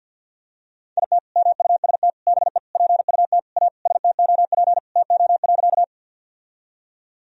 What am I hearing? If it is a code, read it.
Morse code, heard as a keyed tone, about 51 wpm, 700 Hz. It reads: IT MUST BE PUT A STOP TO.
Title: Sentences from Top 1000 Words - Review Part 1 - 50wpm